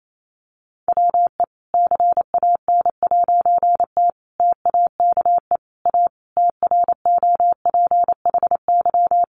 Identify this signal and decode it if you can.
Morse code, heard as a keyed tone, about 28 wpm, 700 Hz. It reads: WE CAN'T TAKE A TROPHY
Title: A Sound Of Thunder 28wpm